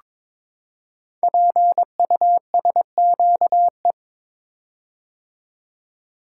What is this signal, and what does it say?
Morse code, heard as a keyed tone, about 22 wpm, 700 Hz. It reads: PUSQE
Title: Code Groups - 5 Letters - 22wpm